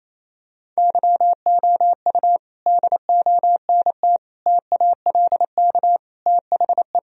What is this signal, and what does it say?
Morse code, heard as a keyed tone, about 27 wpm, 700 Hz. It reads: YOU DONT TALK THE
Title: Sentences from Top 1000 Words - Review Part 3 - 28wpm